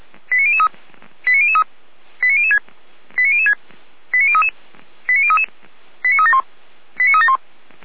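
Five-tone selective call (ZVEI), heard as quick runs of five tones, tones of about 70 ms. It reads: 89003 89003 89006 89006 89033 89033 89371 89371
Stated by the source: ZVEI five-tone selective call received by SDR, from the Feuerwehr Orlamünde fire brigade audio server.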